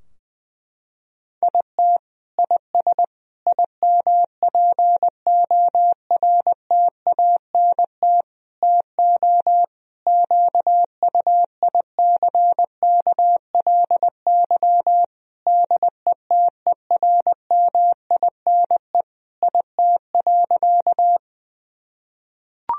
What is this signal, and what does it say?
Morse code, heard as a keyed tone, about 20 wpm, 700 Hz. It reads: IT IS IMPORTANT TO QUICKLY DETERMINE IT.
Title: Sentences from Top 1000 Words - Review Part 4 - 20wpm